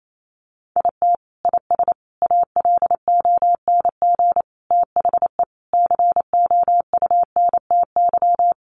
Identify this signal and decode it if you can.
Morse code, heard as a keyed tone, about 28 wpm, 700 Hz. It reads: IT IS ALONG THE COUNTY
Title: Sentences from Top 1200 Words 28wpm